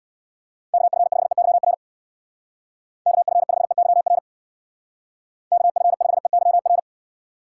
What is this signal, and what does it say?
Morse code, heard as a keyed tone, about 62 wpm, 700 Hz. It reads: KF5EYR KF5EYR KF5EYR
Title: Speed Racing - USA 2x3 Call Signs 50wpm